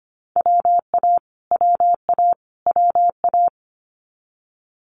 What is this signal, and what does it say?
Morse code, heard as a keyed tone, about 25 wpm, 700 Hz. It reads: WA WA WA
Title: Multi-letter Phonemes Repeated 3x 25wpm